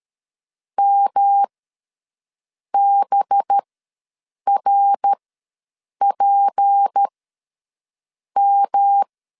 Morse code, heard as a keyed tone, about 13 wpm, spaced as for 6 wpm, 800 Hz. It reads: MBRPM
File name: LEZIONI / 7 GF9Z / 7 GF ws 12wpm 3140